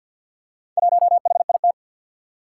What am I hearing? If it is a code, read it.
Morse code, heard as a keyed tone, about 52 wpm, 700 Hz. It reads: 1HIT
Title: Four Challenging Letter-Number Combinations 50wpm